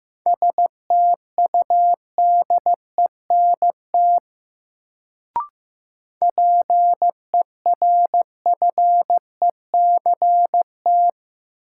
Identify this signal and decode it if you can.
Morse code, heard as a keyed tone, about 15 wpm, 700 Hz. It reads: STUDENT PERFECT
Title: Sentences from Top 1000 Words - New Word Drill - 15wpm